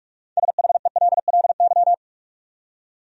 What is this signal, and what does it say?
Morse code, heard as a keyed tone, about 46 wpm, 700 Hz. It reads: SHELLY
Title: USA Common Names - 45wpm